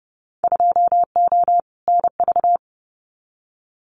Morse code, heard as a keyed tone, about 30 wpm, 700 Hz. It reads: 2O NV
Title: Winter Field Day Exchanges 30wpm